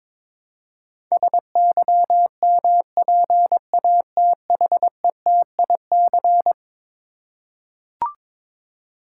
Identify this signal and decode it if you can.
Morse code, heard as a keyed tone, about 22 wpm, 700 Hz. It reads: SYMPATHETIC
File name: practice / 22wpm / Eleven Letter Words - 22wpm